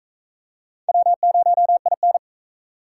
Morse code, heard as a keyed tone, about 42 wpm, 700 Hz. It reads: W0IN